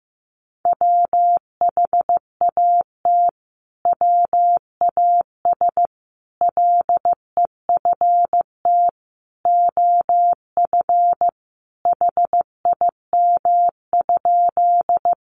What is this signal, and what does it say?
Morse code, heard as a keyed tone, about 14 wpm, 700 Hz. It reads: WHAT WAS LEFT OF HIM?
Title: Sentences from Top 300 Words 15wpm